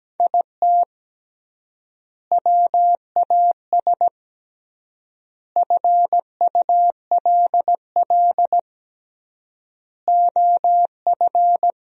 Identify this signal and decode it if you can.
Morse code, heard as a keyed tone, about 17 wpm, 700 Hz. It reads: IT WAS FULL OF
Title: 3x Word-Spacing - Sentences from Top 500 Words - 17wpm